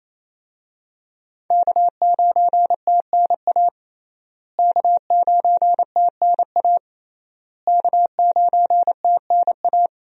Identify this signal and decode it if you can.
Morse code, heard as a keyed tone, about 28 wpm, 700 Hz. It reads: K9TNA K9TNA K9TNA
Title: USA Call Signs - 5 Characters - Repeated 3x - 28wpm